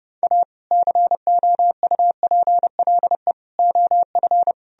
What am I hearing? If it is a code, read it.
Morse code, heard as a keyed tone, about 40 wpm, 700 Hz. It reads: A COUPLE OF